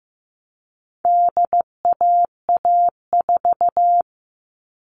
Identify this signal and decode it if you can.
Morse code, heard as a keyed tone, about 15 wpm, 700 Hz. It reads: DAA4
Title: All Cut Numbers - 4-Digit 15wpm